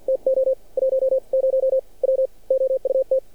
Morse code, heard as a keyed tone, about 50 wpm, spaced as for 42 wpm, 530 Hz. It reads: TO 10 W OUT
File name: archived / CRSnet